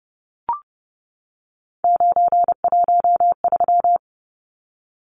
Morse code, heard as a keyed tone, about 30 wpm, 700 Hz. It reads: 913
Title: Numbers - 3-Digit 30wpm